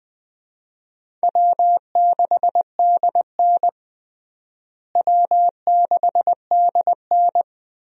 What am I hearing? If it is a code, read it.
Morse code, heard as a keyed tone, about 20 wpm, 700 Hz. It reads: W6DN W6DN
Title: USA Call Signs - 4 Characters - Repeated 2x - 20wpm